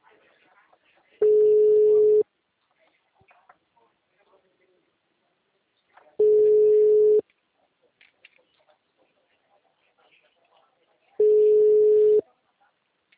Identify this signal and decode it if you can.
Ringback tone (Europe, 425 Hz), with speech.